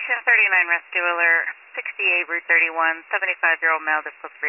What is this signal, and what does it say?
Emergency Alert System (SAME) tones, heard as harsh buzzing bursts, with speech. It is heard from 1.2 to 3.7 s.